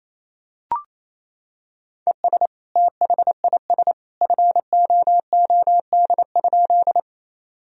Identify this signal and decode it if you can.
Morse code, heard as a keyed tone, about 28 wpm, 700 Hz. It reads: ES THIS FOOD?
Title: Sentences from Top 300 Words and Abbreviations 28wpm